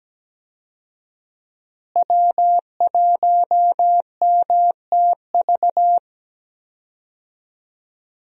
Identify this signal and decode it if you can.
Morse code, heard as a keyed tone, about 17 wpm, 700 Hz. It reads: W1MTV
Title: USA Call Signs - 5 Characters - Repeated 3x - 17wpm